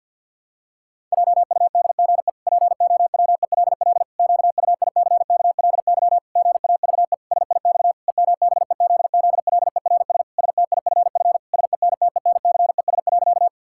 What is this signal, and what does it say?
Morse code, heard as a keyed tone, about 51 wpm, 700 Hz. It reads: JUDGE POWELL QUICKLY GAVE SIX EMBEZZLERS STIFF SENTENCES.
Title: Pangram Challenge v2 - 50wpm